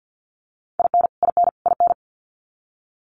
Morse code, heard as a keyed tone, about 48 wpm, 700 Hz. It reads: IN IN IN
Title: Top 200 Words Repeated 50wpm